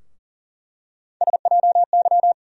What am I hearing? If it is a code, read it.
Morse code, heard as a keyed tone, about 40 wpm, 700 Hz. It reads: SJY
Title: USA Call Sign Postfixes - 3 Letters 40wpm